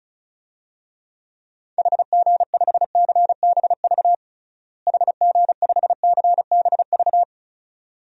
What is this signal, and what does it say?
Morse code, heard as a keyed tone, about 36 wpm, 700 Hz. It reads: HG5CBV HG5CBV
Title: DXCC-100 Call Signs - 6 Characters - Repeated 2x - 35wpm